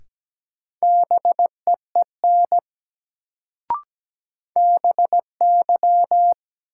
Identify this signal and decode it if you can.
Morse code, heard as a keyed tone, about 17 wpm, 700 Hz. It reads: BEEN BY
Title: Sentences from Top 100 Words - New Word Drill - 17wpm